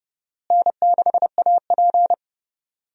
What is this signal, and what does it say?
Morse code, heard as a keyed tone, about 31 wpm, 700 Hz. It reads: N6AP
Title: USA Call Signs - 4 Characters - Repeated 2x - 30wpm